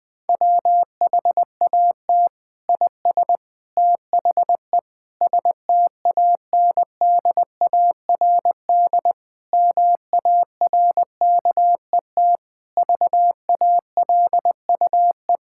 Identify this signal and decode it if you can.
Morse code, heard as a keyed tone, about 20 wpm, 700 Hz. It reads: WHAT IS THE STANDARD MARKET VALUE
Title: Sentences from Top 1000 Words - Review Part 6 - 20wpm